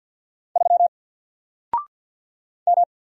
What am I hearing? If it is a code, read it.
Morse code, heard as a keyed tone, about 48 wpm, 700 Hz. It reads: 3 M
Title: Single Letter-Number Alt 50wpm